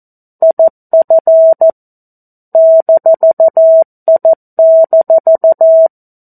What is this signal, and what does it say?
Morse code, heard as a keyed tone, about 14 wpm, 650 Hz. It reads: IF -I-
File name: 02-Chapter-02 14 WPM